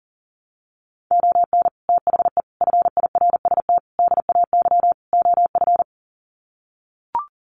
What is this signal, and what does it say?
Morse code, heard as a keyed tone, about 40 wpm, 700 Hz. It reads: ON THE FIRST DAY OF
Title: Sets of 5 Words Encore 40wpm